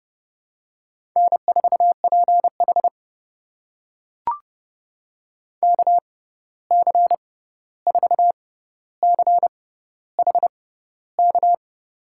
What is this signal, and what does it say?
Morse code, heard as a keyed tone, about 32 wpm, spaced as for 14 wpm, 700 Hz. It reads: N4PH KC4CHK
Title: USA Call Signs - 6x Character Spacing - 30wpm